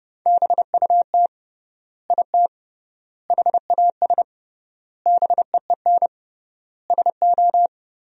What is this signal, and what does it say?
Morse code, heard as a keyed tone, about 30 wpm, 700 Hz. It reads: BUT IT HAS BEEN SO
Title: 3x Word-Spacing - Sentences from Top 100 Words - 30wpm